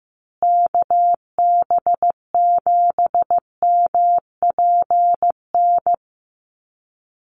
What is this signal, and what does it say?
Morse code, heard as a keyed tone, about 15 wpm, 700 Hz. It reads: KB7MPN
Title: USA Call Signs Repeated 3x 15wpm